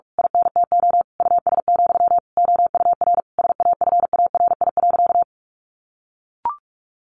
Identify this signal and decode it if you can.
Morse code, heard as a keyed tone, about 44 wpm, 700 Hz. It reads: INTO US, OUR SAFARI.
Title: A Sound Of Thunder 45wpm